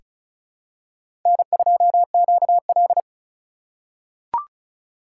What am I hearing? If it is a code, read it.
Morse code, heard as a keyed tone, about 36 wpm, 700 Hz. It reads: N2QL
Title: USA Call Signs - 4 Characters - 3x Character Spacing - 35wpm